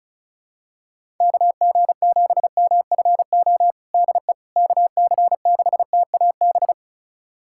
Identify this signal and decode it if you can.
Morse code, heard as a keyed tone, about 36 wpm, 700 Hz. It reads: KG7MFO DE KC6TAB
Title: USA Callsign DE Callsign 35wpm